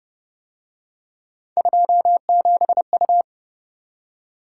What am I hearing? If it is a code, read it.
Morse code, heard as a keyed tone, about 25 wpm, 700 Hz. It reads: 27U
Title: Speed-Racing - Three Letter-Number Combinations - 20wpm